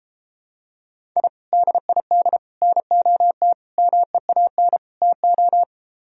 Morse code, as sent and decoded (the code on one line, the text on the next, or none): .. / -.. .. -.. / -. --- - / -- . .- -. / - ---
I DID NOT MEAN TO